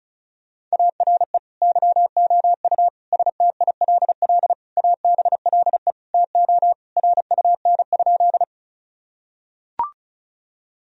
Morse code, heard as a keyed tone, about 34 wpm, 700 Hz. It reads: ARE YOU STILL ABLE TO RUN?
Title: Sentences from Top 1000 Words - Review Part 2 - 35wpm